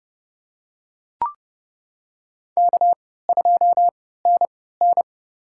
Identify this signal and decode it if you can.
Morse code, heard as a keyed tone, about 30 wpm, spaced as for 17 wpm, 700 Hz. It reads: K2NN
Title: USA Call Signs - 4 Characters - 3x Character Spacing - 30wpm